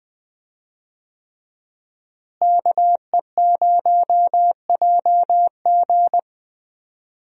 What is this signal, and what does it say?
Morse code, heard as a keyed tone, about 20 wpm, 700 Hz. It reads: KE0JG
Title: USA Call Signs - 5 Characters - Repeated 3x - 20wpm